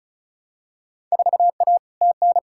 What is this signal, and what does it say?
Morse code, heard as a keyed tone, about 36 wpm, 690 Hz. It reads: 4A TN
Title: Field Day Exchanges Encore - 35wpm